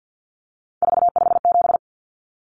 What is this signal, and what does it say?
Morse code, heard as a keyed tone, about 50 wpm, 700 Hz. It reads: 457
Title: Numbers - 3-Digit 50wpm